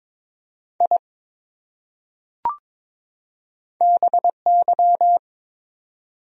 Morse code, heard as a keyed tone, about 22 wpm, 700 Hz. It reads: I BY
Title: Top 100 Words 22wpm